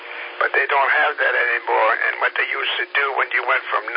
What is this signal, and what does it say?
Emergency Alert System (SAME) tones, heard as harsh buzzing bursts, with speech. It is heard from 0.9 to 3.6 s.